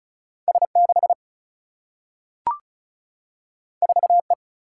Morse code, heard as a keyed tone, about 36 wpm, 700 Hz. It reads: S6 4E
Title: Single-Letter Single-Number 35wpm